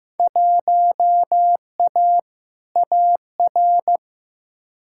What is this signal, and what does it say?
Morse code, heard as a keyed tone, about 15 wpm, 700 Hz. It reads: 1A AR
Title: Field Day Exchanges Encore - 15wpm